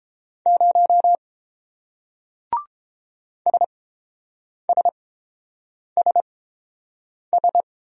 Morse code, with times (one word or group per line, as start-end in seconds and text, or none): 0.5-1.2 0
3.5-7.6 SSSS